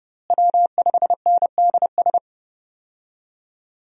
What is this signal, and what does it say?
Morse code, heard as a keyed tone, about 30 wpm, 700 Hz. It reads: W5NDS
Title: USA Call Signs - 5 Characters - Repeated 2x - 30wpm